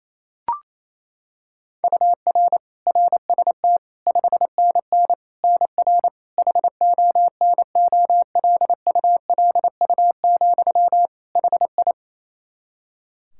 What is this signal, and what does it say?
Morse code, heard as a keyed tone, about 28 wpm, 700 Hz. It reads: UR RST 5NN NR HONOLULU, HI
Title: Practice QSO - Exchange 1 - Beginning - 28wpm